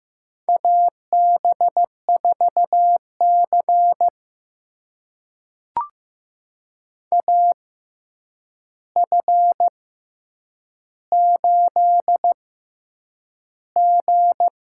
Morse code, heard as a keyed tone, about 15 wpm, spaced as for 8 wpm, 700 Hz. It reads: AB4C AF8G
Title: USA Call Signs - 4 Characters - 6x Character Spacing - 15wpm